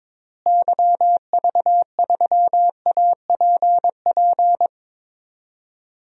Morse code, heard as a keyed tone, about 22 wpm, 700 Hz. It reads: YV3APP
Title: DXCC-100 Call Signs - 6 Characters - Repeated 2x - 22wpm